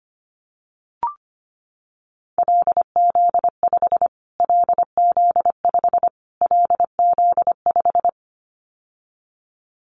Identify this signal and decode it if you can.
Morse code, heard as a keyed tone, about 25 wpm, 700 Hz. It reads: LZ5 LZ5 LZ5
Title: Callsign Prefixes for Top 25 DXCC - Repeated 3x - 25wpm